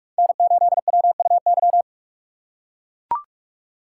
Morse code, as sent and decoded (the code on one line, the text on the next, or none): -. ---.. .-- ..- -.--
N8WUY